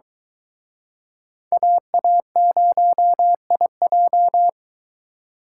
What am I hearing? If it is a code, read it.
Morse code, heard as a keyed tone, about 24 wpm, 700 Hz. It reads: AA0IJ